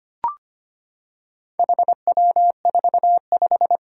Morse code, heard as a keyed tone, about 25 wpm, 700 Hz. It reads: HW45
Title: Four Challenging Letter-Number Combinations 25wpm